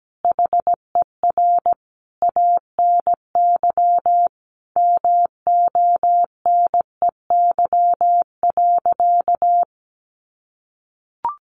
Morse code, heard as a keyed tone, about 17 wpm, 700 Hz. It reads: HER ANY MONEY.
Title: Sentences from Top 1500 Words 17wpm